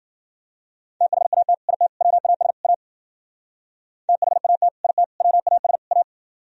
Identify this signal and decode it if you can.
Morse code, heard as a keyed tone, about 56 wpm, 700 Hz. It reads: THAT IT WAS A THAT IT WAS A